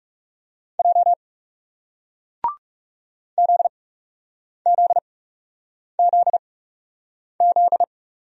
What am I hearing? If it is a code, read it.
Morse code, heard as a keyed tone, about 40 wpm, spaced as for 8 wpm, 700 Hz. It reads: J ZZZZ